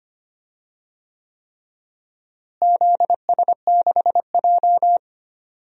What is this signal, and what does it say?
Morse code, heard as a keyed tone, about 25 wpm, 700 Hz. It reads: ZS6J
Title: DXCC-100 Call Signs - 4 Characters - Repeated 3x - 25wpm